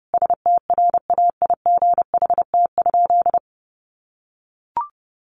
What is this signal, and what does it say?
Morse code, heard as a keyed tone, about 30 wpm, 700 Hz. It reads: STRAIGHT?
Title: Sentences from Top 1000 Words 30wpm